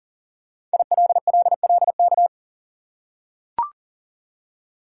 Morse code, heard as a keyed tone, about 40 wpm, 700 Hz. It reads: ILLLK
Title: Code Groups - 5 Letters - 40wpm